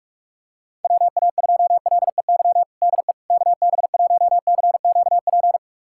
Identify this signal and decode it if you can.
Morse code, heard as a keyed tone, about 41 wpm, 690 Hz. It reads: WA2LEY DE KB1CQP
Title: USA Callsign DE Callsign 45wpm